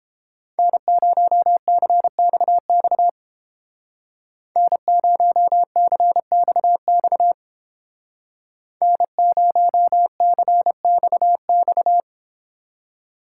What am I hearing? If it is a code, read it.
Morse code, heard as a keyed tone, about 29 wpm, 700 Hz. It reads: N0CXX N0CXX N0CXX